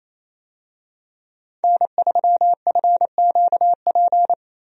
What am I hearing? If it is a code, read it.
Morse code, heard as a keyed tone, about 27 wpm, 700 Hz. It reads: N3FQP
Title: USA Call Signs - 5 Characters - 28wpm